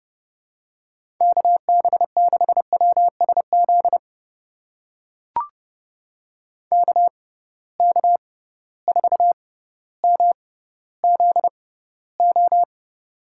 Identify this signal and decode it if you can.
Morse code, heard as a keyed tone, about 30 wpm, spaced as for 15 wpm, 700 Hz. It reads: KB6WSZ KK4MZO
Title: USA Call Signs - 6 Characters - 6x Character Spacing - 30wpm